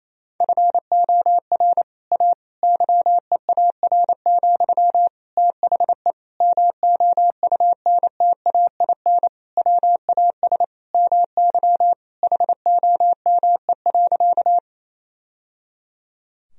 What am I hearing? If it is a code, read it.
Morse code, heard as a keyed tone, about 28 wpm, 700 Hz. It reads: FOR A YEAR, THE MOUNTAIN WAS MY HOME.
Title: Sentences from Top 1000 Words - Review Part 1 - 28wpm